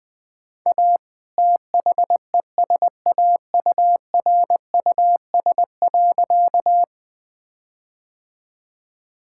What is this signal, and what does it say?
Morse code, heard as a keyed tone, about 21 wpm, 700 Hz. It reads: A THESAURUS.
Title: All-Age Appropriate Jokes - 20wpm